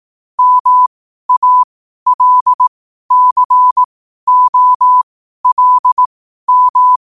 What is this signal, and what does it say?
Morse code, heard as a keyed tone, about 18 wpm, spaced as for 12 wpm, 1000 Hz. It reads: MALCOLM
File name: qotdfiles / 12wpm